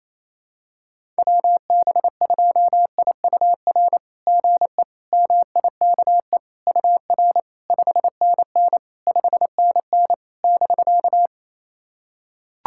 Morse code, heard as a keyed tone, about 28 wpm, 700 Hz. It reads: WB2IUR GE MIKE UR 5NN 5NN <BK>
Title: SOTA QSO Practice 28wpm